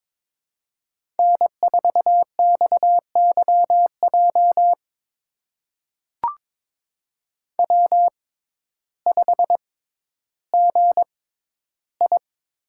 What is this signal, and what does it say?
Morse code, heard as a keyed tone, about 23 wpm, spaced as for 12 wpm, 700 Hz. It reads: N4XYJ W5GI
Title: USA Call Signs - 6x Character Spacing - 22wpm